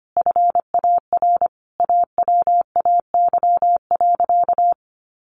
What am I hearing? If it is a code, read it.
Morse code, heard as a keyed tone, about 25 wpm, 700 Hz. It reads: FAR AWAY.
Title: Sentences from Top 1500 Words 25wpm